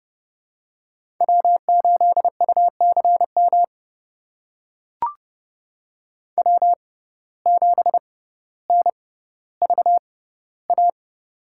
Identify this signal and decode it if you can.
Morse code, heard as a keyed tone, about 30 wpm, spaced as for 15 wpm, 700 Hz. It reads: W8UCM W7NVA
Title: USA Call Signs - 5 Characters - 6x Character Spacing - 30wpm